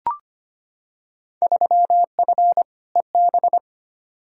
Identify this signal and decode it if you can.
Morse code, heard as a keyed tone, about 25 wpm, 700 Hz. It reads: 3F EB
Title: Field Day Exchanges - 25wpm